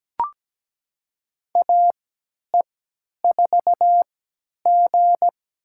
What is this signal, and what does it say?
Morse code, heard as a keyed tone, about 17 wpm, spaced as for 10 wpm, 700 Hz. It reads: AE4G
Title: USA Call Signs - 4 Characters - 3x Character Spacing - 17wpm